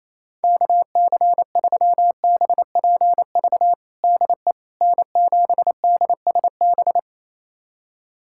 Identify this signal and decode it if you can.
Morse code, heard as a keyed tone, about 28 wpm, 700 Hz. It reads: KC3BPV DE N7DSB
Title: USA Callsign DE Callsign 28wpm